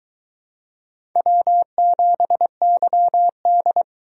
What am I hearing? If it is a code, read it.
Morse code, heard as a keyed tone, about 23 wpm, 700 Hz. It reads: W7YD